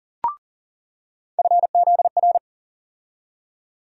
Morse code, heard as a keyed tone, about 40 wpm, 700 Hz. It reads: FZR
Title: USA Call Sign Postfixes - 3 Letters 40wpm